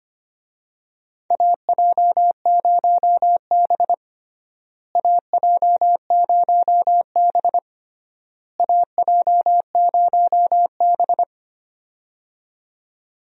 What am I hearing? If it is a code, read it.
Morse code, heard as a keyed tone, about 25 wpm, 700 Hz. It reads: AJ0B AJ0B AJ0B
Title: USA Call Signs - 4 Characters - Repeated 3x - 25wpm